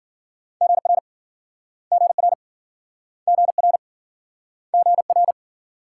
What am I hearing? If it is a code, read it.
Morse code, heard as a keyed tone, about 53 wpm, 700 Hz. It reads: GR GR GR GR